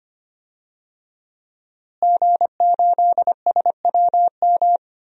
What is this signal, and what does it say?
Morse code, heard as a keyed tone, about 25 wpm, 700 Hz. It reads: G8SWM
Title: DXCC-100 Call Signs - 5 Characters - Repeated 3x - 25wpm